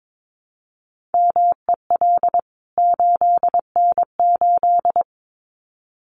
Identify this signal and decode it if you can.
Morse code, heard as a keyed tone, about 22 wpm, 700 Hz. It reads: MEL 8N8
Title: CWT Exchange 22wpm